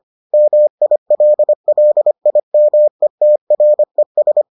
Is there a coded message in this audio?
Yes: Morse code from 0.3 to 4.4 s, heard as a keyed tone, about 25 wpm, 600 Hz. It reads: MILLIMETRES